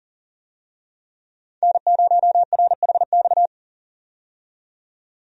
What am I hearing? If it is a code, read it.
Morse code, heard as a keyed tone, about 40 wpm, 700 Hz. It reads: N0RHX
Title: USA Call Signs - 5 Characters - 40wpm